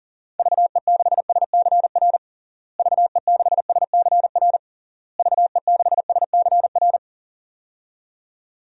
Morse code, heard as a keyed tone, about 40 wpm, 700 Hz. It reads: VE6SCR VE6SCR VE6SCR
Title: DXCC-100 Call Signs - 6 Characters - Repeated 3x - 40wpm